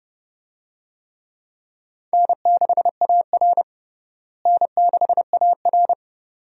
Morse code, heard as a keyed tone, about 30 wpm, 700 Hz. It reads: N6AR N6AR
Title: USA Call Signs - 4 Characters - Repeated 2x - 30wpm